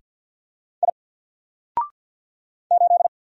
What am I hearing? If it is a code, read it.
Morse code, heard as a keyed tone, about 50 wpm, 700 Hz. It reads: I 8
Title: Single Letter-Number Alt 50wpm